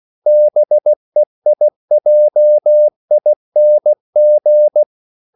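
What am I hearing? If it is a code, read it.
Morse code, heard as a keyed tone, about 16 wpm, 600 Hz. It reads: BEIJING
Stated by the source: CW practice of national capitals at 16 words per minute, from ON6ZQ.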